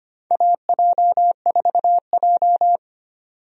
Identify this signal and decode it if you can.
Morse code, heard as a keyed tone, about 25 wpm, 700 Hz. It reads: AJ4J
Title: USA Call Signs - 4 Characters - 25wpm